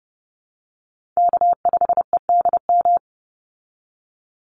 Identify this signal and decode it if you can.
Morse code, heard as a keyed tone, about 30 wpm, 700 Hz. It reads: K5EDM
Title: Call Signs Repeated 3x - 30wpm